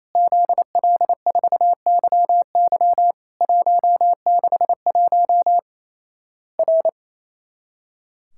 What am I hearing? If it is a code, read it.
Morse code, heard as a keyed tone, about 28 wpm, 700 Hz. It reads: ZL4YY 161 R
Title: Worked All Europe DX Contest - QTC Exchange - 28wpm